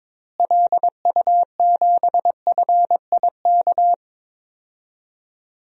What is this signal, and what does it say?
Morse code, heard as a keyed tone, about 22 wpm, 700 Hz. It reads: LU7FIK
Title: DXCC-100 Call Signs - 6 Characters - Repeated 2x - 22wpm